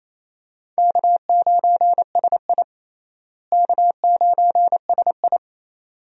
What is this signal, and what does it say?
Morse code, heard as a keyed tone, about 27 wpm, 700 Hz. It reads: K9SI K9SI
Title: USA Call Signs - 4 Characters - Repeated 2x - 28wpm